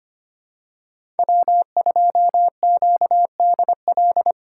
Morse code, heard as a keyed tone, about 25 wpm, 700 Hz. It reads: W2QDL
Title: USA Call Signs - 5 Characters - 4x Character Spacing - 25wpm